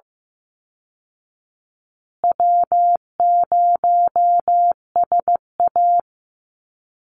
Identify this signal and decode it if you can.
Morse code, heard as a keyed tone, about 15 wpm, 700 Hz. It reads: W0SA
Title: Easy Call Signs Repeated 2x - 15wpm